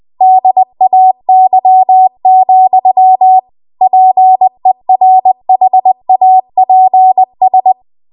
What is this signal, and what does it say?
Morse code, heard as a keyed tone, about 20 wpm, 750 Hz. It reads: DAY, PERHAPS